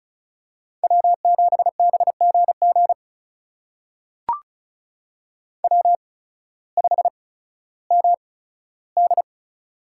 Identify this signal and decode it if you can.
Morse code, heard as a keyed tone, about 36 wpm, spaced as for 16 wpm, 700 Hz. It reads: W7BGG W5MD
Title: USA Call Signs - 8x Character Spacing - 35wpm